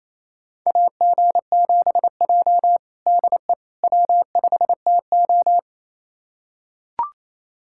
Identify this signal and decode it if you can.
Morse code, heard as a keyed tone, about 28 wpm, 700 Hz. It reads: AG7J DE W5TO
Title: USA Callsign DE Callsign - 4 Characters - 28wpm